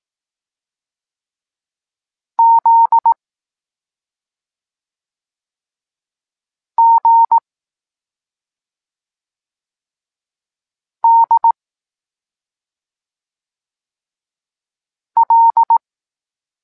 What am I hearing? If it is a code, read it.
Morse code, heard as a keyed tone, about 18 wpm, 920 Hz. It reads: Z G D L
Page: Tekst 23.3 – Ciąg dalszy doskonalenia odbioru poznanych liter